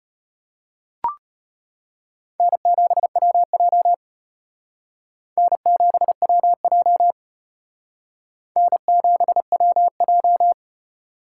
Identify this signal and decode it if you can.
Morse code, heard as a keyed tone, about 32 wpm, 700 Hz. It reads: N7WJ N7WJ N7WJ